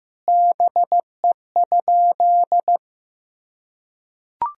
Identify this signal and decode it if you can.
Morse code, heard as a keyed tone, about 15 wpm, 700 Hz. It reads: BE?
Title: Sentences from Top 100 Words and Abbreviations 15wpm